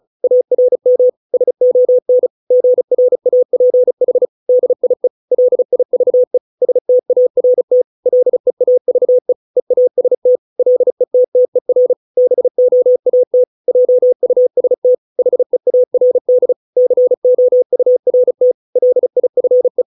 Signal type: Morse code